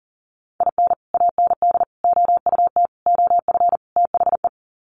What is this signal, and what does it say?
Morse code, heard as a keyed tone, about 40 wpm, 700 Hz. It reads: IN AND OUT OF THE
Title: Sets of 5 Words 40wpm